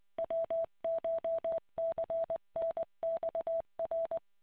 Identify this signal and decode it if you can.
Morse code, heard as a keyed tone, about 20 wpm, 660 Hz. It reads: WCHCNXR